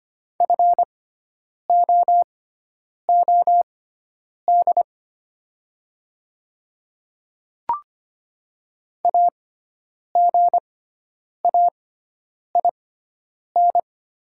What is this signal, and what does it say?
Morse code, heard as a keyed tone, about 25 wpm, spaced as for 9 wpm, 690 Hz. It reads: FOOD AGAIN
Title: 6x Character Spacing - Top 300 Words - 25wpm